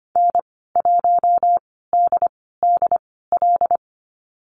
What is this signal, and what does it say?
Morse code, heard as a keyed tone, about 25 wpm, spaced as for 16 wpm, 700 Hz. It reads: N1DDL
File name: Lesson 15g - Number 5 - Call Signs 25wpm